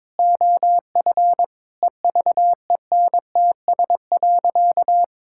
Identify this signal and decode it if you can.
Morse code, heard as a keyed tone, about 22 wpm, 700 Hz. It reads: OF EVENTS.
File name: practice / 22wpm / Sentences with 8-Word Phrases - 22wpm